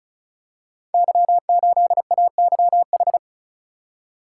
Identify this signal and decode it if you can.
Morse code, heard as a keyed tone, about 36 wpm, 700 Hz. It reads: Y8AYH
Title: Code Groups - 5 Letter:Numbers - 35wpm